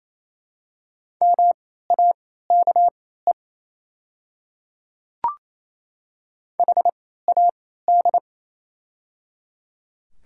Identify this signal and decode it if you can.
Morse code, heard as a keyed tone, about 28 wpm, spaced as for 16 wpm, 700 Hz. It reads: MAKE HAD
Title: 3x Character Spacing - Top 300 Words - 28wpm